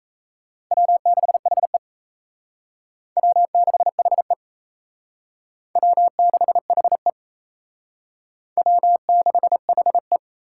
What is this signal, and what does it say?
Morse code, heard as a keyed tone, about 35 wpm, 700 Hz. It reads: W6HE W6HE W6HE W6HE
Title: Speed Racing - USA 1x2 Call Signs 28wpm